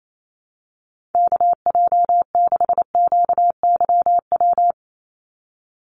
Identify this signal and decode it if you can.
Morse code, heard as a keyed tone, about 28 wpm, 700 Hz. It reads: KJ6QYW
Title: USA Call Signs 28wpm